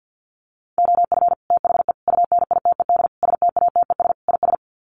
Morse code, heard as a keyed tone, about 50 wpm, 700 Hz. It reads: OF THE UNITED STATES IS
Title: Sets of 5 Words Encore2 50wpm